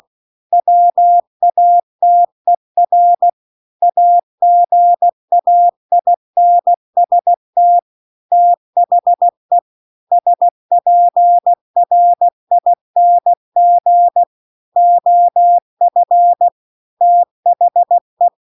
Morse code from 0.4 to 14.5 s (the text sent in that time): WATER AGAINST THE SPRING